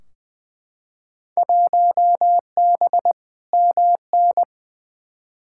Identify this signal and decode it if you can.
Morse code, heard as a keyed tone, about 20 wpm, 700 Hz. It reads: 1B MN
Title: Field Day Exchanges - 20wpm